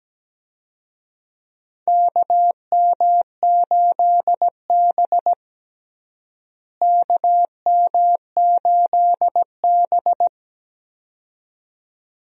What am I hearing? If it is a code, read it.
Morse code, heard as a keyed tone, about 17 wpm, 700 Hz. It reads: KM8B KM8B
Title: USA Call Signs - 4 Characters - Repeated 2x - 17wpm